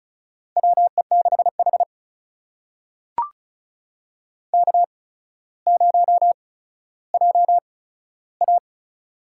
Morse code, heard as a keyed tone, about 34 wpm, spaced as for 15 wpm, 700 Hz. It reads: WE6H K0JA
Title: USA Call Signs - 4 Characters - 8x Character Spacing - 35wpm